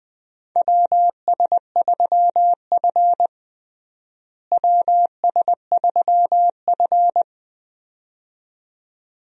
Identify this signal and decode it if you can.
Morse code, heard as a keyed tone, about 21 wpm, 700 Hz. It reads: WS3F WS3F
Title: USA Call Signs - 4 Characters - Repeated 2x - 20wpm